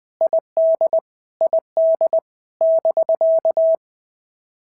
Morse code, heard as a keyed tone, about 19 wpm, 650 Hz. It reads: ID ID <BK>
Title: POTA QSO Practice - 20wpm